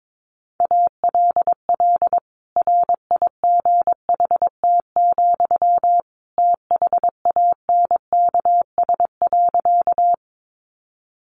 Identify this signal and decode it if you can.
Morse code, heard as a keyed tone, about 22 wpm, 700 Hz. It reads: ALL RIGHT, THANKS.
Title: Sentences from Top 600 Words 22wpm